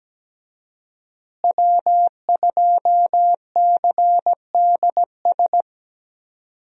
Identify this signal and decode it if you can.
Morse code, heard as a keyed tone, about 17 wpm, 700 Hz. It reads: W2CDS
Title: USA Call Signs - 5 Characters - Repeated 3x - 17wpm